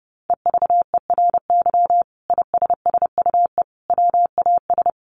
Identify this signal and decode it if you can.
Morse code, heard as a keyed tone, about 28 wpm, 700 Hz. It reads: EVERY ISSUE WAS
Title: Sentences from Top 1400 Words 30wpm